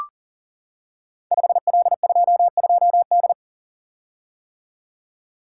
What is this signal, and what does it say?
Morse code, heard as a keyed tone, about 42 wpm, 700 Hz. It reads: 5L22D
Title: Code Groups - 5 Letter:Numbers - 40wpm